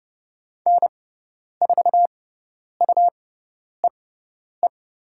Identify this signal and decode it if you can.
Morse code, heard as a keyed tone, about 30 wpm, spaced as for 10 wpm, 690 Hz. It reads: N4UEE
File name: Lesson 18g - Letter m - Call Signs 30wpm